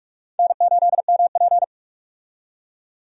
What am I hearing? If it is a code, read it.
Morse code, heard as a keyed tone, about 42 wpm, 700 Hz. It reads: N8MP
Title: USA Call Signs - 4 Characters - 45wpm